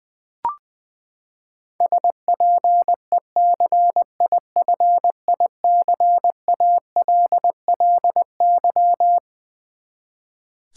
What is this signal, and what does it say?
Morse code, heard as a keyed tone, about 21 wpm, 700 Hz. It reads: SPECIFICALLY
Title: Sentences from Top 1100 Words - New Word Drill v2 - 20wpm